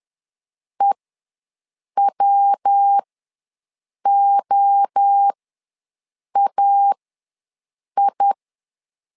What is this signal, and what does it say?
Morse code, heard as a keyed tone, about 10 wpm, spaced as for 6 wpm, 800 Hz. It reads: EWOAI
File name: LEZIONI / 3 MWRO / 3 MWRO ws 10wpm 2310